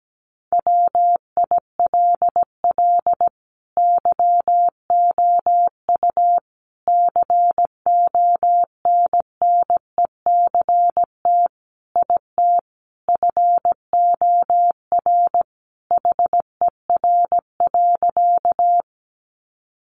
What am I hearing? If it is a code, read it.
Morse code, heard as a keyed tone, about 17 wpm, 700 Hz. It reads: WILL YOU CONNECT IT FOR HER.
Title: Sentences from Top 1500 Words 17wpm